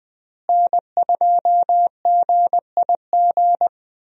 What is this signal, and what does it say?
Morse code, heard as a keyed tone, about 20 wpm, 700 Hz. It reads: N2GIG
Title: USA Call Signs - 5 Characters - Repeated 3x - 20wpm